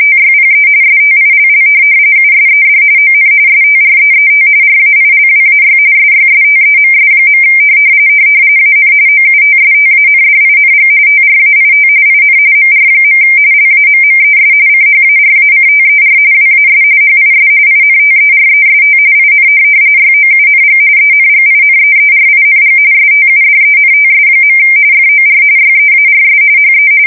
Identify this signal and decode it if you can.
RTTY, about 45.45 baud. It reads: TYPE STILL USED ON STREET RAILWAY SYSTEMS. AN OUTSTANDING TRAIT IN MR. LAMMES CHARACTER WAS HIS INTEREST IN YOUNG ENGINEERING STUDENTS. FOR YEARS HE